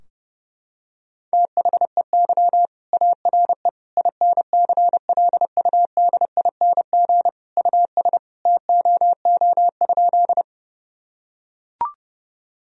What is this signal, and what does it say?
Morse code, heard as a keyed tone, about 30 wpm, 700 Hz. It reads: THEY ARE INCLUDING US TOO?
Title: Sentences from Top 1000 Words - Review Part 4 - 30wpm